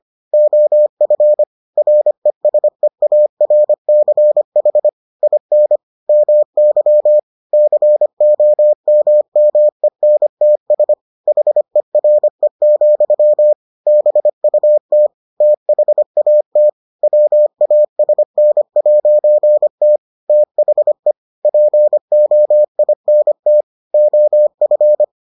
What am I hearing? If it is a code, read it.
Morse code, heard as a keyed tone, about 25 wpm, 600 Hz. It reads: OF RESEARCH IN MY COMMENTS HERE, BUT THAT WASN'T THE POINT OF